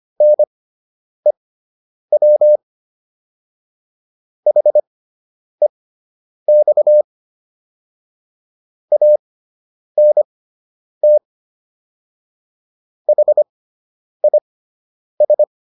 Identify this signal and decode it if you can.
Morse code, heard as a keyed tone, about 25 wpm, spaced as for 9 wpm, 600 Hz. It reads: NEW HEX ANT HIS